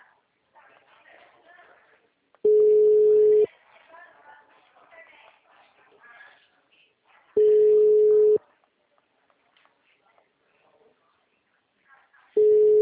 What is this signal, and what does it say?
Ringback tone (Europe, 425 Hz), with speech.
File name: out- -2017- - - .13504